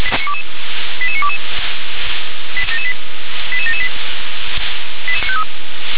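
Five-tone selective call (ZVEI), heard as quick runs of five tones, tones of about 70 ms. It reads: 88922 88778
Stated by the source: SDR-received ZVEI five-tone selective call, code 88952.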